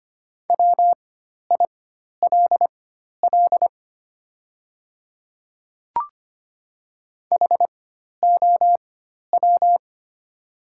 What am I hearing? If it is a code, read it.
Morse code, heard as a keyed tone, about 25 wpm, spaced as for 12 wpm, 690 Hz. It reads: WILL HOW
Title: 4x Character Spacing - Top 100 Words - 25wpm